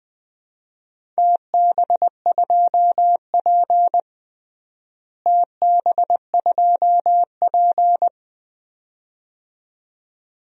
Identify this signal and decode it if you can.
Morse code, heard as a keyed tone, about 21 wpm, 700 Hz. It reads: TB2P TB2P
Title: DXCC-100 Call Signs - 4 Characters - Repeated 2x - 20wpm